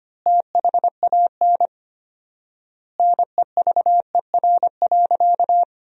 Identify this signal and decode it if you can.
Morse code, heard as a keyed tone, about 25 wpm, 700 Hz. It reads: THAN NEVER.
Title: 4x Word-Spacing - Sentences from Top 400 Words - 25wpm